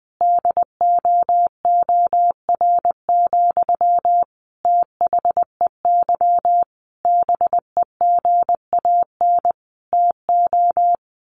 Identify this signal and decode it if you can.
Morse code, heard as a keyed tone, about 19 wpm, 700 Hz. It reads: DOOR, THEY BEGAN TO
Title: Sentences from Top 1400 Words 20wpm